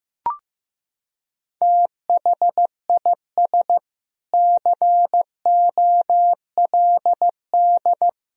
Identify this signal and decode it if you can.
Morse code, heard as a keyed tone, about 16 wpm, 700 Hz. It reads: THIS COLD
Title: Sentences from Top 1000 Words - Review Part 2 - 15wpm